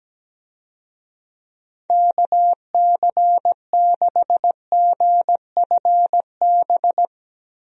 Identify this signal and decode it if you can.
Morse code, heard as a keyed tone, about 17 wpm, 700 Hz. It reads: KC6GFB
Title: USA Call Signs - 6 Characters - 17wpm